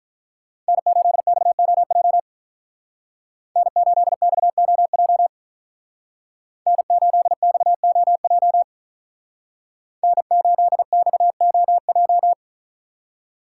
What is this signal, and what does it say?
Morse code, heard as a keyed tone, about 46 wpm, 700 Hz. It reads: N8XOJ N8XOJ N8XOJ N8XOJ